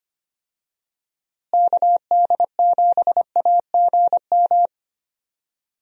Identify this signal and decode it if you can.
Morse code, heard as a keyed tone, about 25 wpm, 700 Hz. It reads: KD7AGM
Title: USA Call Signs - 6 Characters - Repeated 2x - 25wpm